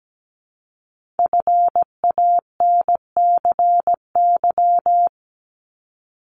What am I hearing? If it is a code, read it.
Morse code, heard as a keyed tone, about 17 wpm, 700 Hz. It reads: FANCY
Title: Multi-letter Phonemes Repeated 3x 17wpm